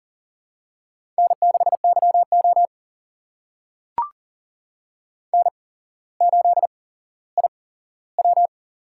Morse code, heard as a keyed tone, about 40 wpm, spaced as for 17 wpm, 690 Hz. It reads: N6YO N8IW
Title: USA Call Signs - 8x Character Spacing - 40wpm